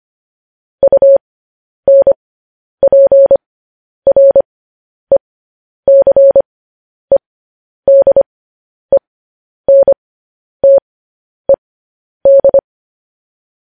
Morse code, heard as a keyed tone, about 25 wpm, spaced as for 10 wpm, 550 Hz. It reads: UNPRECEDENTED
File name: podcast / News headlines-1x-25x10